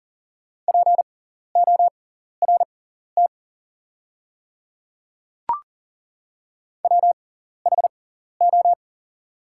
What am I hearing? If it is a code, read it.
Morse code, heard as a keyed tone, about 40 wpm, spaced as for 14 wpm, 690 Hz. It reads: PORT WHO
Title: 6x Character Spacing - Top 300 Words - 40wpm